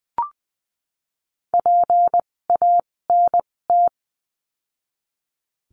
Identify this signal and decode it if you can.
Morse code, heard as a keyed tone, about 20 wpm, spaced as for 16 wpm, 700 Hz. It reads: PANT
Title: 256 Lesson 42a - Words 12-20wpm